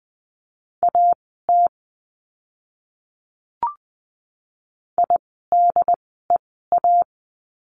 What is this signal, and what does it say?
Morse code, heard as a keyed tone, about 20 wpm, spaced as for 15 wpm, 700 Hz. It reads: AT IDEA
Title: 108 Lesson 18f - Letter m - Words 20wpm